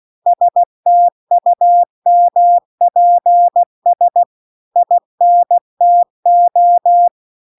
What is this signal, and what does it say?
Morse code, heard as a keyed tone, about 16 wpm, 700 Hz. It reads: STUMPS INTO